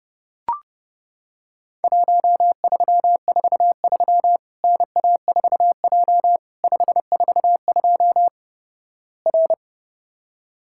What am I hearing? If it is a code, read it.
Morse code, heard as a keyed tone, about 29 wpm, 700 Hz. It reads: 1343 NA4J 542 R
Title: Worked All Europe DX Contest - QTC Exchange - 30wpm